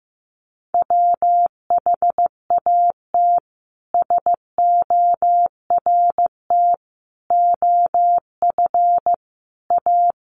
Morse code, heard as a keyed tone, about 15 wpm, 700 Hz. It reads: WHAT SORT OF A
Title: Sentences from Top 1300 Words 15wpm